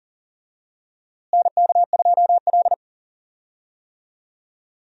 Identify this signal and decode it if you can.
Morse code, heard as a keyed tone, about 40 wpm, 700 Hz. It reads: NK2L
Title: USA Call Signs - 4 Characters - Repeated 3x - 40wpm